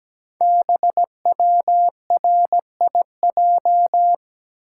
Morse code, heard as a keyed tone, about 17 wpm, 700 Hz. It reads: BWRIJ
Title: Code Groups - 5 Letters - 17wpm